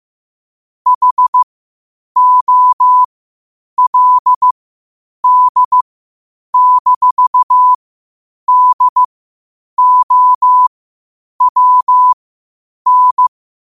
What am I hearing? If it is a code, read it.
Morse code, heard as a keyed tone, about 15 wpm, spaced as for 8 wpm, 1000 Hz. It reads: HOLD-DOWN
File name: qotdfiles / 8wpm